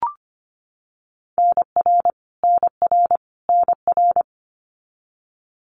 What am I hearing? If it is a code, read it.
Morse code, heard as a keyed tone, about 25 wpm, 700 Hz. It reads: NR NR NR
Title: Abbreviations Repeated 3x - 25wpm